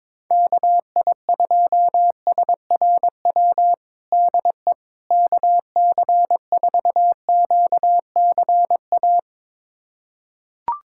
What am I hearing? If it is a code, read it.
Morse code, heard as a keyed tone, about 22 wpm, 700 Hz. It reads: KI2SRW DE KC4QCA
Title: USA Callsign DE Callsign 22wpm